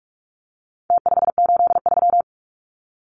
Morse code, heard as a keyed tone, about 44 wpm, 700 Hz. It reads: T583
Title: Numbers - Contest Style - 4-Digit 45wpm